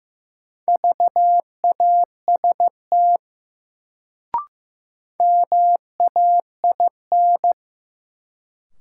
Morse code, heard as a keyed tone, about 15 wpm, 700 Hz. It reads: VAST MAIN
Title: Sentences from Top 1300 Words - New Word Drill Alt - 15wpm